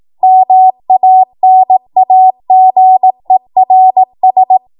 Morse code, heard as a keyed tone, about 18 wpm, 750 Hz. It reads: MANAGERS
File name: Morse / 18WPM